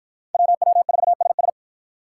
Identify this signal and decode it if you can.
Morse code, heard as a keyed tone, about 53 wpm, 700 Hz. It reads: WW4SS